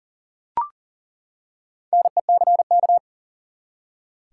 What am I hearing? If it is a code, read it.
Morse code, heard as a keyed tone, about 40 wpm, 700 Hz. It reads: NECK
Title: Four Letter Words - 40wpm